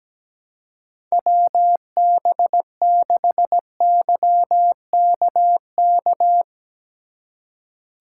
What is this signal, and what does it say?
Morse code, heard as a keyed tone, about 17 wpm, 700 Hz. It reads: WB6YKK
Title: USA Call Signs - 4x Character Spacing - 17wpm